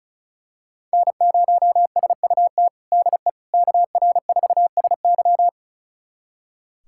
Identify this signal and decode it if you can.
Morse code, heard as a keyed tone, about 36 wpm, 700 Hz. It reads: N0SUT DE KR4SY
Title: USA Callsign DE Callsign - 5 Characters - 35wpm